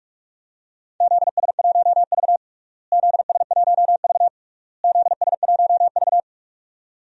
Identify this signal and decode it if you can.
Morse code, heard as a keyed tone, about 46 wpm, 700 Hz. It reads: ZS1V ZS1V ZS1V
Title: DXCC-100 Call Signs - 4 Characters - Repeated 3x - 45wpm